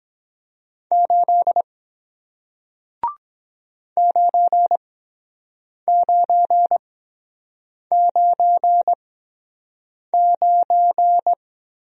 Morse code, heard as a keyed tone, about 23 wpm, spaced as for 7 wpm, 700 Hz. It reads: 8 9999